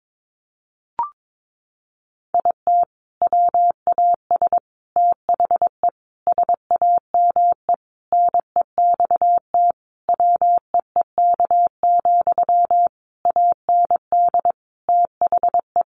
Morse code, heard as a keyed tone, about 22 wpm, 700 Hz. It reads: IT WAS THE SAME NEXT WEEK, AND THE
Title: Sentences from Top 500 Words 22wpm